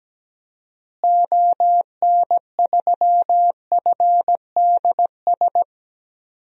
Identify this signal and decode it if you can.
Morse code, heard as a keyed tone, about 17 wpm, 700 Hz. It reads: ON3FDS
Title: DXCC-100 Call Signs - 6 Characters - Repeated 2x - 17wpm